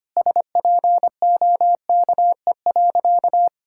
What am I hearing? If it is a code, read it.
Morse code, heard as a keyed tone, about 25 wpm, 700 Hz. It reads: SPOKE.
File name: practice / 25wpm / Sentences with 4-Word Phrases - 25wpm